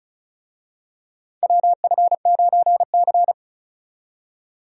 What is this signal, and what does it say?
Morse code, heard as a keyed tone, about 36 wpm, 690 Hz. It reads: WF9C
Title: USA Call Signs - 4 Characters - 35wpm